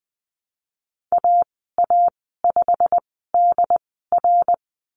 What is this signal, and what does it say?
Morse code, heard as a keyed tone, about 20 wpm, spaced as for 15 wpm, 700 Hz. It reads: AA5DR
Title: 088 Lesson 15g - Number 5 - Call Signs 20wpm